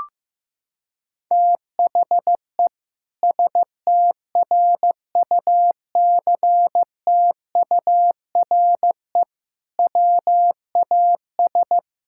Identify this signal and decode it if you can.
Morse code, heard as a keyed tone, about 15 wpm, 700 Hz. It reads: THE STRUCTURE WAS
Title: Sentences from Top 1000 Words - Review Part 6 - 15wpm